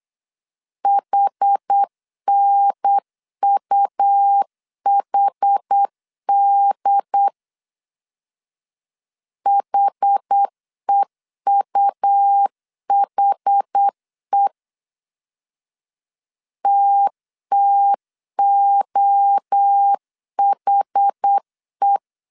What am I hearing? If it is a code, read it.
Morse code, heard as a keyed tone, about 8 wpm, 800 Hz. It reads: HNUHD HEUHE TTOHE